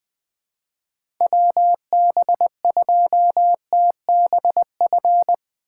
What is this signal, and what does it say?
Morse code, heard as a keyed tone, about 20 wpm, 700 Hz. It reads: WB2TBF
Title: USA Call Signs - 6x Character Spacing - 20wpm